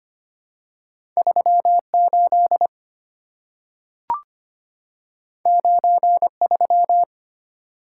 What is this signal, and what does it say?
Morse code, heard as a keyed tone, about 25 wpm, 700 Hz. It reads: 38 93
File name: Two Letter-Number Combinations - 25wpm